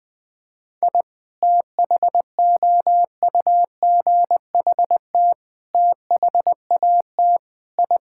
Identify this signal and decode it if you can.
Morse code, heard as a keyed tone, about 19 wpm, 700 Hz. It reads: I THOUGHT THAT I
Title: Sentences with 3-Word Phrases - Spoken Beginning - 20wpm